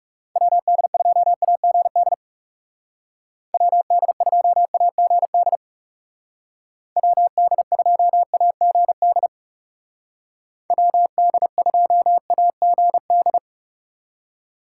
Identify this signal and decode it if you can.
Morse code, heard as a keyed tone, about 37 wpm, 700 Hz. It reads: WD2AGD WD2AGD WD2AGD WD2AGD
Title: Speed Racing - USA Call Signs 30wpm